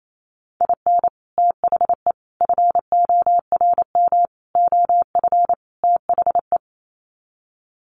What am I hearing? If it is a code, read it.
Morse code, heard as a keyed tone, about 28 wpm, 700 Hz. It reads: IN THE FORM OF THE
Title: Sets of 5 Words Encore2 28wpm